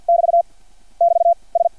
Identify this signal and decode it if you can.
Morse code, heard as a keyed tone, about 50 wpm, 680 Hz. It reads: <BT> <BT> S
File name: archived / CRSnet